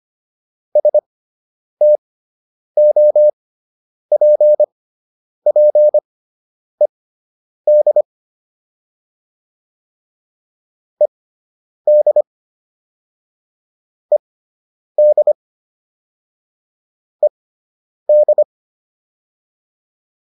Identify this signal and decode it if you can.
Morse code, heard as a keyed tone, about 25 wpm, spaced as for 9 wpm, 600 Hz. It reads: STOPPED ED ED ED